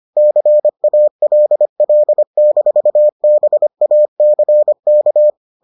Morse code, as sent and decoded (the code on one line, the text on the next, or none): -.-. .- .-.. .-.. -....- -... .- -.-. -.-
CALL-BACK